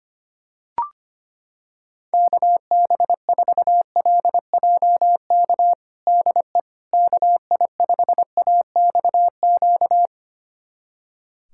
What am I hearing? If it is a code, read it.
Morse code, heard as a keyed tone, about 25 wpm, 700 Hz. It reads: KB4LJK DE KI5AXQ
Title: USA Callsign DE Callsign - 6 Characters - 25wpm